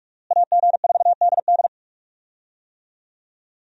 Morse code, heard as a keyed tone, about 41 wpm, 700 Hz. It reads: AG4DD
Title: USA Call Signs - 5 Characters - Repeated 3x - 45wpm